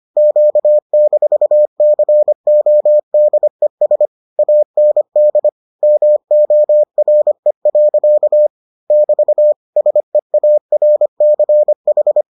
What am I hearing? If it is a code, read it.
Morse code, heard as a keyed tone, about 25 wpm, 600 Hz. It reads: Q-CODES AND MORE. <BT> SEARCH